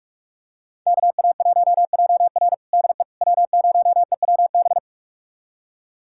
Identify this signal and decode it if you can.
Morse code, heard as a keyed tone, about 41 wpm, 700 Hz. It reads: KA1JR DE W0EWB
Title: USA Callsign DE Callsign - 5 Characters - 45wpm